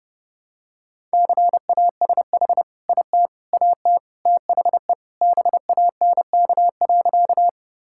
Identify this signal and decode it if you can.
Morse code, heard as a keyed tone, about 28 wpm, 700 Hz. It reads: CASH IT AT THE BANK.
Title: Sentences from Top 1000 Words - Review Part 5 - 30wpm